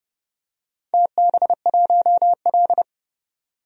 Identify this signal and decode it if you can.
Morse code, heard as a keyed tone, about 31 wpm, 700 Hz. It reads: TB1L
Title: DXCC-100 Call Signs - 4 Characters - 30wpm